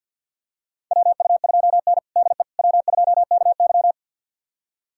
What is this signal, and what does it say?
Morse code, heard as a keyed tone, about 53 wpm, 700 Hz. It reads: WU2N DE W2KY
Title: USA Callsign DE Callsign - 4 Characters - 50wpm